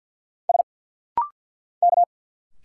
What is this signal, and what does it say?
Morse code, heard as a keyed tone, about 50 wpm, 690 Hz. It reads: S K
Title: Single Letters - Rapid-Fire 50wpm